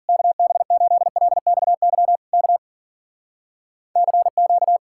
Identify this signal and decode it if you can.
Morse code, heard as a keyed tone, about 53 wpm, 700 Hz. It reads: KB8LXY K CQ